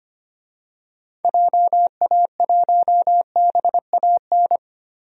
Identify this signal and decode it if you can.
Morse code, heard as a keyed tone, about 25 wpm, 700 Hz. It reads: JA1BAN
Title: DXCC-100 Call Signs - 6 Characters - Repeated 2x - 25wpm